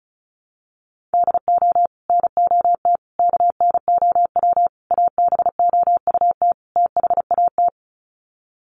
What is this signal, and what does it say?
Morse code, heard as a keyed tone, about 34 wpm, 700 Hz. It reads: DO NOT KNOW ABOUT THAT
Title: Sets of 5 Words Encore2 35wpm